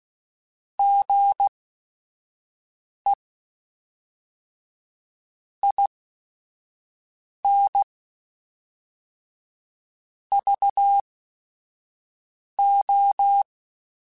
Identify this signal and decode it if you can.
Morse code, heard as a keyed tone, about 16 wpm, spaced as for 5 wpm, 800 Hz. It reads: GE IN VO
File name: Technical Subjects / Technical Subjects-mp3s / XC @5